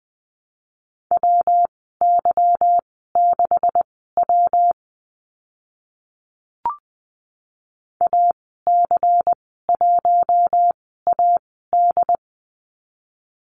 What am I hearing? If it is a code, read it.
Morse code, heard as a keyed tone, about 20 wpm, spaced as for 14 wpm, 700 Hz. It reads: WY6W AC1AD
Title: 167 Lesson 27g - Number 7 - Call Signs 20wpm